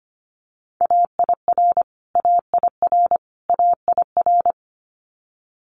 Morse code, heard as a keyed tone, about 25 wpm, 700 Hz. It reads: AIR AIR AIR
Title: Multi-letter Phonemes Repeated 3x 25wpm